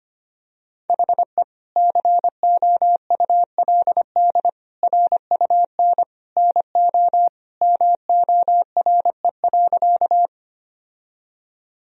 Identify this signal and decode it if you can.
Morse code, heard as a keyed tone, about 25 wpm, 700 Hz. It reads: HE COULD RUN NO MORE.
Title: Sentences from Top 1000 Words - Review Part 1 - 25wpm